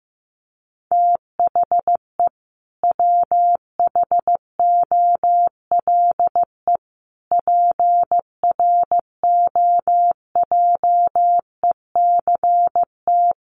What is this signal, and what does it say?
Morse code, heard as a keyed tone, about 15 wpm, 700 Hz. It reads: THE WHOLE PROJECT
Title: Sentences from Top 1400 Words 15wpm